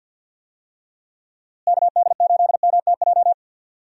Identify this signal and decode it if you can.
Morse code, heard as a keyed tone, about 51 wpm, 700 Hz. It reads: KD8MTJ
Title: USA Call Signs - 6 Characters - Repeated 3x - 50wpm